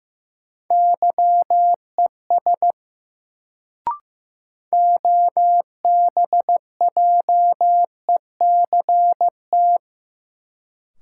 Morse code, heard as a keyed tone, about 15 wpm, 700 Hz. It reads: YES OBJECT
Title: Sentences from Top 500 Words - New Word Drill - 15wpm